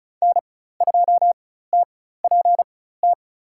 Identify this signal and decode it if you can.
Morse code, heard as a keyed tone, about 36 wpm, spaced as for 16 wpm, 700 Hz. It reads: N2TPT
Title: USA Call Signs - 5 Characters - 4x Character Spacing - 35wpm